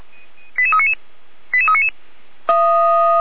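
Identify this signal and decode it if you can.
Five-tone selective call (ZVEI), heard as quick runs of five tones, tones of about 70 ms. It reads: 88399 88399
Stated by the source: ZVEI five-tone call received by SDR.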